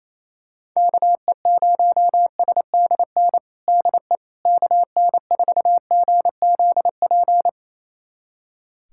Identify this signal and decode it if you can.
Morse code, heard as a keyed tone, about 28 wpm, 700 Hz. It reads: KE0SDN DE KN4GZP
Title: USA Callsign DE Callsign - 6 Characters - 28wpm